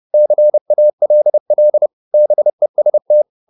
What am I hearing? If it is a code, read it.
Morse code, heard as a keyed tone, about 30 wpm, 600 Hz. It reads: CALL BEST